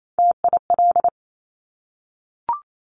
Morse code, heard as a keyed tone, about 28 wpm, 700 Hz. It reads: TIL
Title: Top Three Letter Combinations in Words 28wpm